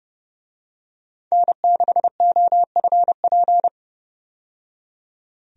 Morse code, heard as a keyed tone, about 30 wpm, 700 Hz. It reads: N6OFP
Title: USA Call Signs - 5 Characters - 30wpm